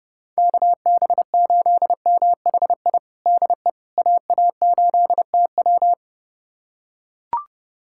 Morse code, heard as a keyed tone, about 32 wpm, 700 Hz. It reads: KB8MHI DE AA8TW
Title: USA Callsign DE Callsign 30wpm